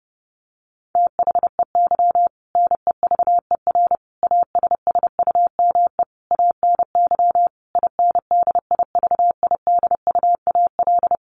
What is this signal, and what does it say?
Morse code, heard as a keyed tone, about 30 wpm, 700 Hz. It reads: THEY NEVER ASSUME ANY INDIVIDUAL
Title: Sentences from Top 1500 Words 30wpm